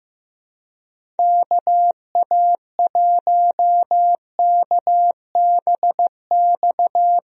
Morse code, heard as a keyed tone, about 15 wpm, 700 Hz. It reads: KA1KBX
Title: USA Call Signs - 6 Characters - Repeated 2x - 15wpm